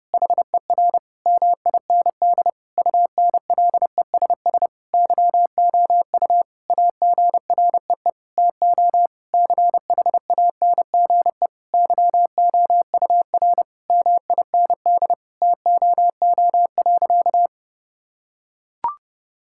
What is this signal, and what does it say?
Morse code, heard as a keyed tone, about 30 wpm, 700 Hz. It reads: HER MIND UNLESS YOU AGREE TO CHANGE YOUR MIND TOO.
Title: Sentences from Top 1000 Words - Review Part 6 - 30wpm